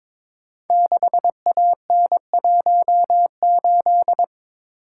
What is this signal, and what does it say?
Morse code, heard as a keyed tone, about 23 wpm, 700 Hz. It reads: 6AN18
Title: Code Groups - 5 Letter:Numbers - 22wpm